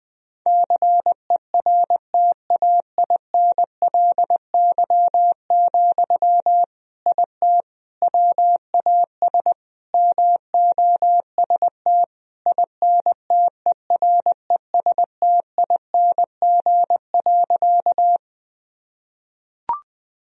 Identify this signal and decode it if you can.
Morse code, heard as a keyed tone, about 20 wpm, 700 Hz. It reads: CERTAINLY, IT WAS MOST INTERESTING.
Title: Sentences from Top 1000 Words - Review Part 1 - 20wpm